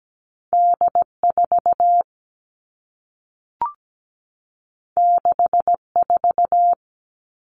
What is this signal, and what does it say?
Morse code, heard as a keyed tone, about 17 wpm, 700 Hz. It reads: D 4 6 4
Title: Two Challenging Letter-Number Combinations 17wpm